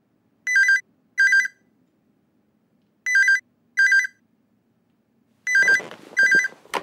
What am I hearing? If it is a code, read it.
There is no code.